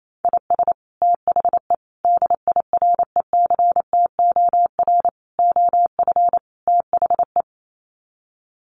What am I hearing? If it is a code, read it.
Morse code, heard as a keyed tone, about 28 wpm, 700 Hz. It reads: IS THE DIRECTOR OF THE
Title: Sets of 5 Words Encore2 28wpm